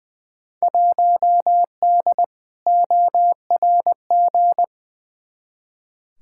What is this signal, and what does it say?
Morse code, heard as a keyed tone, about 20 wpm, 700 Hz. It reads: 1D ORG
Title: Field Day Exchanges - 20wpm